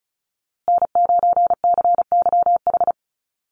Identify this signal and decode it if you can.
Morse code, heard as a keyed tone, about 34 wpm, 700 Hz. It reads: N9CYH
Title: USA Call Signs Repeated 3x 35wpm